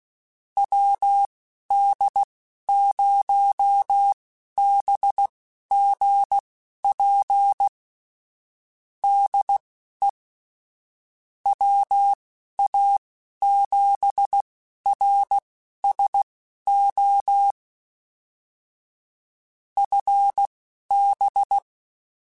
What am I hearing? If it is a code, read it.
Morse code, heard as a keyed tone, about 16 wpm, spaced as for 11 wpm, 800 Hz. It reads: WD0BGP DE WA7RSO FB
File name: QSOs / QSOs-mp3s / WD0BGP 1@13